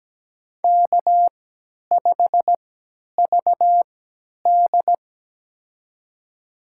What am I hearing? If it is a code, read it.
Morse code, heard as a keyed tone, about 17 wpm, spaced as for 10 wpm, 700 Hz. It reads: K5VD
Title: USA Call Signs - 4 Characters - 3x Character Spacing - 17wpm